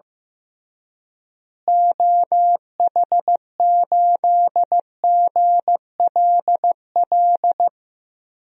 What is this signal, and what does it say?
Morse code, heard as a keyed tone, about 15 wpm, 700 Hz. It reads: OH8GLL
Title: DXCC-100 Call Signs - 6 Characters - Repeated 3x - 15wpm